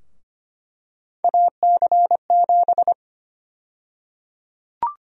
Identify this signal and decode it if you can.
Morse code, heard as a keyed tone, about 25 wpm, 700 Hz. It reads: AC7
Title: USA Call Sign Prefixes - 25wpm